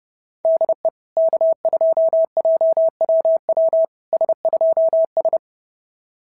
Morse code, heard as a keyed tone, about 28 wpm, 650 Hz. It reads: DE K2JWW S2S
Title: SOTA QSO Practice 30wpm